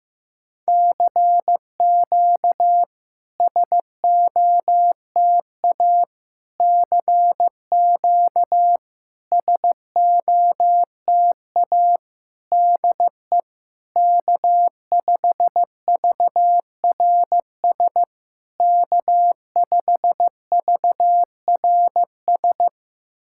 Morse code, heard as a keyed tone, about 16 wpm, 700 Hz. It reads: CQ SOTA CQ SOTA DE K5VRS K5VRS
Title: SOTA QSO Practice 15wpm